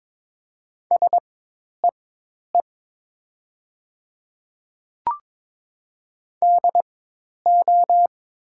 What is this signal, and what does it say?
Morse code, heard as a keyed tone, about 21 wpm, spaced as for 10 wpm, 700 Hz. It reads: SEE DO
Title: 4x Character Spacing - Top 300 Words - 22wpm